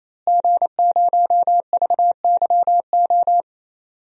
Morse code, heard as a keyed tone, about 27 wpm, 700 Hz. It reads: G0VYO
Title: DXCC-100 Call Signs - 5 Characters - Repeated 3x - 28wpm